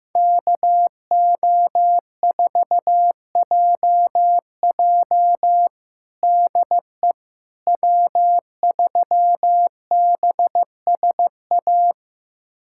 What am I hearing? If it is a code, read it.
Morse code, heard as a keyed tone, about 15 wpm, 700 Hz. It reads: KO4JJ DE W3BSA
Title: USA Callsign DE Callsign - 5 Characters - 15wpm